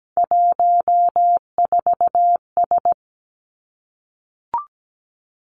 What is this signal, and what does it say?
Morse code, heard as a keyed tone, about 17 wpm, 700 Hz. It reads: 14S
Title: Three Challenging Letter-Number Combinations 17wpm